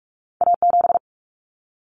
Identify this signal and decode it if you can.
Morse code, heard as a keyed tone, about 46 wpm, 700 Hz. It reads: A7
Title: Common Cut Numbers - 2-Digit 45wpm